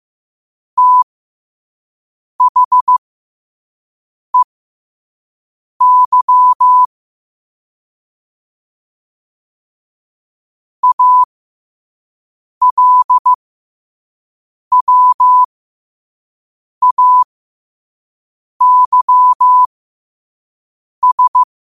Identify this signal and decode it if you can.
Morse code, heard as a keyed tone, about 15 wpm, spaced as for 5 wpm, 1000 Hz. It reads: THEY ALWAYS
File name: 5wpm